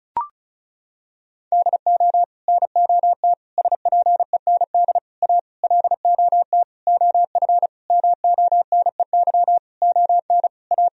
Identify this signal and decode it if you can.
Morse code, heard as a keyed tone, about 34 wpm, 700 Hz. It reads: DO NOT SPEND A LOT OF MONEY ON A
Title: Sentences from Top 1000 Words - Review Part 4 - 35wpm